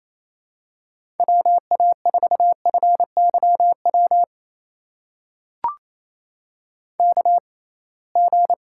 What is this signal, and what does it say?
Morse code, heard as a keyed tone, about 27 wpm, spaced as for 21 wpm, 700 Hz. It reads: WA4FYW KG
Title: USA Call Signs - 6 Characters - 6x Character Spacing - 28wpm